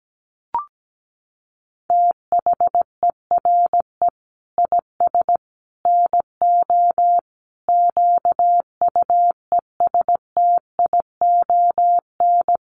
Morse code, heard as a keyed tone, about 17 wpm, 700 Hz. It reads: THERE IS NO QUESTION
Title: Sets of 5 Words Encore 17wpm